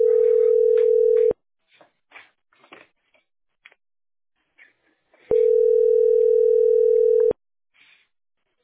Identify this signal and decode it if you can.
Ringback tone (North America), 440 + 480 Hz.
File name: be887f4a-0bf2-4b38-b68b